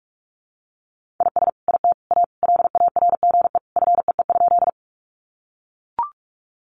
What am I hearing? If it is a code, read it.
Morse code, heard as a keyed tone, about 44 wpm, 700 Hz. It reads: IS IT A LARGE FEE?
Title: Sentences from Top 1500 Words 45wpm